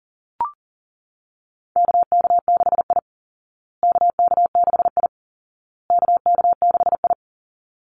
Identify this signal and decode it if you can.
Morse code, heard as a keyed tone, about 40 wpm, 700 Hz. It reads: KK6I KK6I KK6I
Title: Call Signs Repeated 2x - 40wpm